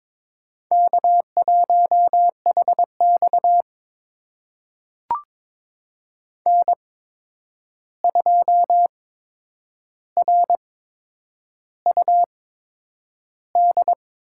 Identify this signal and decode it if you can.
Morse code, heard as a keyed tone, about 22 wpm, spaced as for 9 wpm, 700 Hz. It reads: K1HX N2RUD
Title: USA Call Signs - 8x Character Spacing - 22wpm